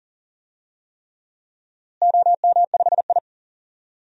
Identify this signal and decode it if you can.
Morse code, heard as a keyed tone, about 40 wpm, 700 Hz. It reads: OM5I
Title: DXCC-100 Call Signs - 4 Characters - Repeated 3x - 40wpm